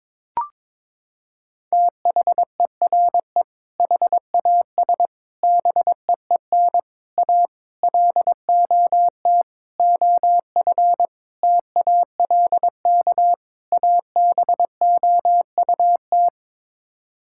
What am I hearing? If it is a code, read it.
Morse code, heard as a keyed tone, about 22 wpm, 700 Hz. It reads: THERE HAS BEEN A LOT OF TALK ABOUT
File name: practice / 22wpm / Sets of 8 Words - 22wpm